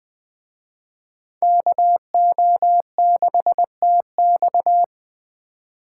Morse code, heard as a keyed tone, about 20 wpm, 700 Hz. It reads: KO6TX
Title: Speed Racing - USA 2x2 Call Signs 17wpm